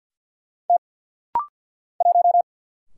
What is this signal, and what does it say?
Morse code, heard as a keyed tone, about 50 wpm, 700 Hz. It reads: T 1
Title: Single Letter-Number Alt - Rapid-Fire 50wpm